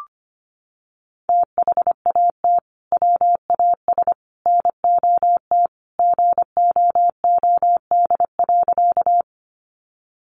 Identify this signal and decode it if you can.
Morse code, heard as a keyed tone, about 25 wpm, 700 Hz. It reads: THAT WAS NOT GOOD.
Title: Sentences from Top 600 Words 25wpm